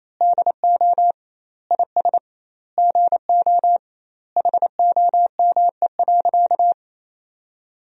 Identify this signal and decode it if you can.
Morse code, heard as a keyed tone, about 28 wpm, 700 Hz. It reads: DO IS GO HOME.
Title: 2x Word-Spacing - Sentences from Top 200 Words - 28wpm